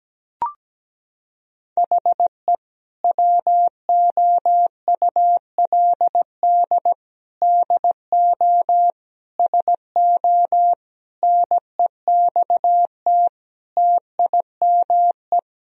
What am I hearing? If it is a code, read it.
Morse code, heard as a keyed tone, about 17 wpm, 700 Hz. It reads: HE WOULD DO SO NEXT TIME
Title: Sentences from Top 1000 Words - Review Part 1 - 17wpm